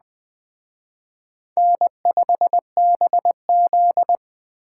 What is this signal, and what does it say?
Morse code, heard as a keyed tone, about 20 wpm, 700 Hz. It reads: N5BZ
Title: Speed Racing - USA 1x2 Call Signs 17wpm